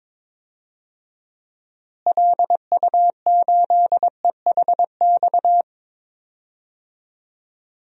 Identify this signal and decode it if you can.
Morse code, heard as a keyed tone, about 22 wpm, 700 Hz. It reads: LU8EHX
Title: DXCC-100 Call Signs - 6 Characters - Repeated 3x - 22wpm